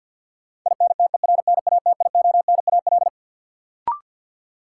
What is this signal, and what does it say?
Morse code, heard as a keyed tone, about 53 wpm, 700 Hz. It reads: INTERNATIONAL
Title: Sentences from Top 1000 Words - Review Part 4 - New Word Drill - 50wpm